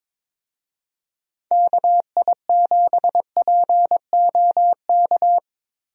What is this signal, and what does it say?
Morse code, heard as a keyed tone, about 22 wpm, 700 Hz. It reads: KI7POK
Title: USA Call Signs - 6 Characters - Repeated 3x - 22wpm